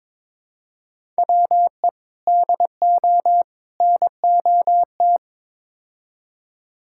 Morse code, heard as a keyed tone, about 22 wpm, 700 Hz. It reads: WE DO NOT
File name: practice / 22wpm / Build A Phrase - 5-Word Phrases - 22wpm